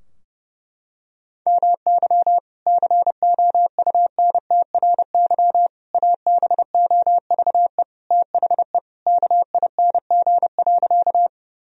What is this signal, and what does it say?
Morse code, heard as a keyed tone, about 32 wpm, 700 Hz. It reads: MY COUNTRY ABOVE THE KING.
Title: Sentences from Top 1000 Words - Review Part 2 - 30wpm